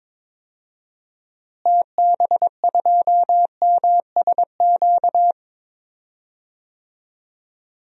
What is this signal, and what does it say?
Morse code, heard as a keyed tone, about 22 wpm, 700 Hz. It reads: TB2MSQ
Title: DXCC-100 Call Signs - 6 Characters - Repeated 3x - 22wpm